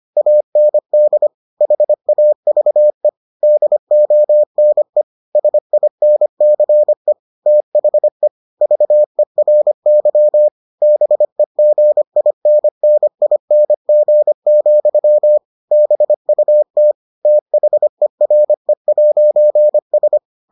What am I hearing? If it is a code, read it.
Morse code, heard as a keyed tone, about 25 wpm, 600 Hz. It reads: AND HAVE DONE SINCE THE VERY BEGINNING, BUT THERE'S